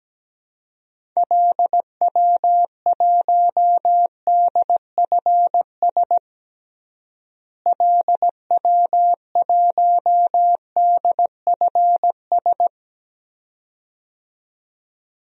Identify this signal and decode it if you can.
Morse code, heard as a keyed tone, about 17 wpm, 700 Hz. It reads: LW1DFS LW1DFS
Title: DXCC-100 Call Signs - 6 Characters - Repeated 2x - 17wpm